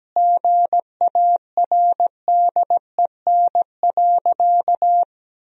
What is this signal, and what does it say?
Morse code, heard as a keyed tone, about 17 wpm, 700 Hz. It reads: GARDEN.
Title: Sentences from Top 1000 Words - Review Part 4 - 17wpm